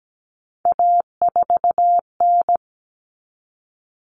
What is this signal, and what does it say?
Morse code, heard as a keyed tone, about 17 wpm, 690 Hz. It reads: A4N
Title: Common Cut Numbers - 3-Digit 17wpm